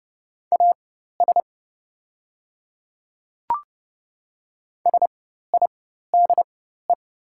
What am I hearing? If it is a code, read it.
Morse code, heard as a keyed tone, about 30 wpm, spaced as for 14 wpm, 690 Hz. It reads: AS SIDE
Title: 4x Character Spacing - Top 100 Words - 30wpm